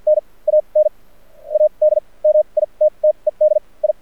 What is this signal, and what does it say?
Morse code, heard as a keyed tone, about 50 wpm, spaced as for 33 wpm, 610 Hz. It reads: N AN ADMITTED I
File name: archived / CRSnet